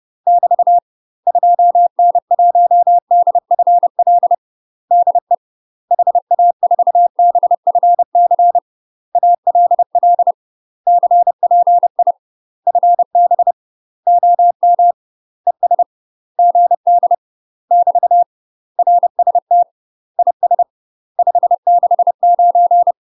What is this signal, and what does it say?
Morse code, heard as a keyed tone, about 30 wpm, 700 Hz. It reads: <BT> 2N1DFL DE HA4BFC ALL CPI FB OM ES GD <BT> RST IS 569
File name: UK MASTER QSOs TRAINING / 30WPM / UK MASTER QSOs 0038